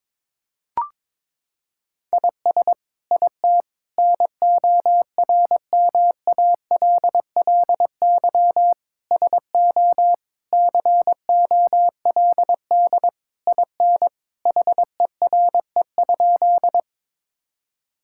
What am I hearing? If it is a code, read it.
Morse code, heard as a keyed tone, about 22 wpm, 700 Hz. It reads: IS IT NORMALLY SO COLD IN HERE?
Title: Sentences from Top 1000 Words - Review Part 6 - 22wpm